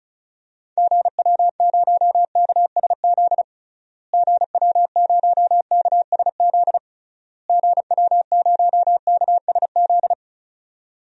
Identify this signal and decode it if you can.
Morse code, heard as a keyed tone, about 36 wpm, 700 Hz. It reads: GW0KSZ GW0KSZ GW0KSZ
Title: DXCC-100 Call Signs - 6 Characters - Repeated 3x - 35wpm